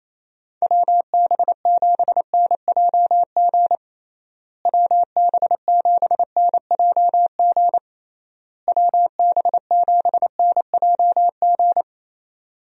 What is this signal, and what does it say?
Morse code, heard as a keyed tone, about 28 wpm, 700 Hz. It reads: WB7NJG WB7NJG WB7NJG
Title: USA Call Signs - 6 Characters - Repeated 3x - 28wpm